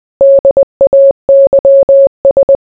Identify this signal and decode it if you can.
Morse code, heard as a keyed tone, about 20 wpm, 550 Hz. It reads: DAYS